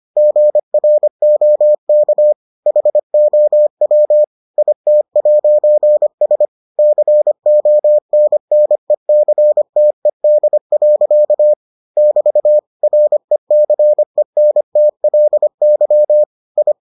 Morse code, heard as a keyed tone, about 25 wpm, 600 Hz. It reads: GROK HOW IT'S CONNECTED. <BT> RECENTLY I